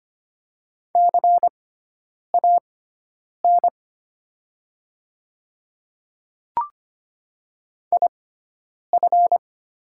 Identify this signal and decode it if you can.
Morse code, heard as a keyed tone, about 25 wpm, spaced as for 9 wpm, 700 Hz. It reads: CAN IF
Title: 6x Character Spacing - Top 100 Words - 25wpm